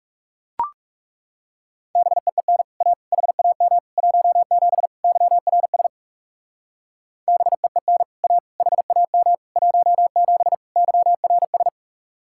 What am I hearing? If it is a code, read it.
Morse code, heard as a keyed tone, about 36 wpm, 700 Hz. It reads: BEEN A HAM 17 YRS BEEN A HAM 17 YRS
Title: Speed-Racing Common QSO Phrases - 30wpm